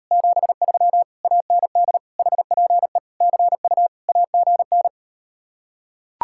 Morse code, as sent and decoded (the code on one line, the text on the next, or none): --... ...-- / .- -. -.. / .... .--. / -.-. ..- / .- --. -.
73 AND HPE CU AGN